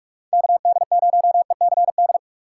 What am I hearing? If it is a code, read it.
Morse code, heard as a keyed tone, about 44 wpm, 700 Hz. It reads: KD0ECD